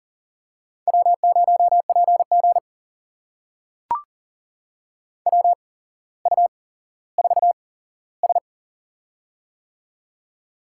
Morse code, heard as a keyed tone, about 38 wpm, spaced as for 17 wpm, 700 Hz. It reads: W0PG WU4S
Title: USA Call Signs - 8x Character Spacing - 40wpm